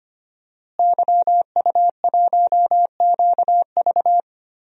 Morse code, heard as a keyed tone, about 25 wpm, 700 Hz. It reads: YU1QV
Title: DXCC-100 Call Signs - 5 Characters - Repeated 2x - 25wpm